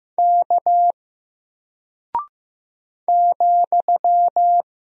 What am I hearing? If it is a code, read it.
Morse code, heard as a keyed tone, about 15 wpm, 700 Hz. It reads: K ,
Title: Single Character (Letter:Number:Punctuation:Prosign) 15wpm